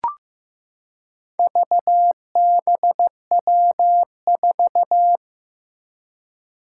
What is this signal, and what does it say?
Morse code, heard as a keyed tone, about 16 wpm, 700 Hz. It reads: VBW4
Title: Four Challenging Letter-Number Combinations 15wpm